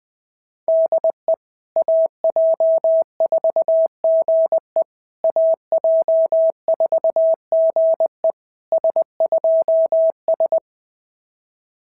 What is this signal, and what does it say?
Morse code, heard as a keyed tone, about 21 wpm, 690 Hz. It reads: DE AJ4GE AJ4GE S2S
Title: SOTA QSO Practice 20wpm